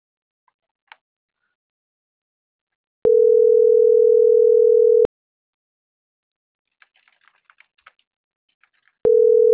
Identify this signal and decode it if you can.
Ringback tone (North America), 440 + 480 Hz.